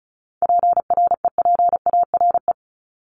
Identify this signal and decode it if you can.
Morse code, heard as a keyed tone, about 34 wpm, 700 Hz. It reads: PREPARE
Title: ED Suffix - Top 100 Words with Base Words 35wpm